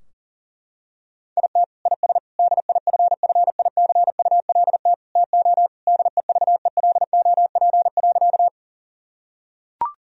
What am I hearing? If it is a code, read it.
Morse code, heard as a keyed tone, about 40 wpm, 700 Hz. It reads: IT IS DIFFICULT TO DEVELOP.
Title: Sentences from Top 1000 Words - Review Part 2 - 40wpm